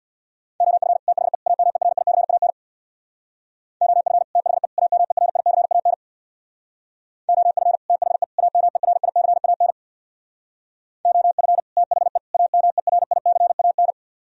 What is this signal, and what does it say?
Morse code, heard as a keyed tone, about 64 wpm, 700 Hz. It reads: OF THE AMERICAN OF THE AMERICAN OF THE AMERICAN OF THE AMERICAN